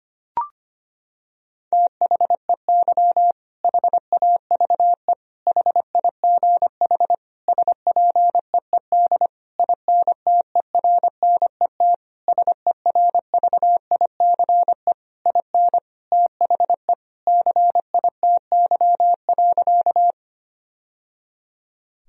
Morse code, heard as a keyed tone, about 25 wpm, 700 Hz. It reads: THEY HAVE HIGH SPEED INTERNET SERVICE IN THE CITY.
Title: Sentences from Top 1000 Words - Review Part 4 - 25wpm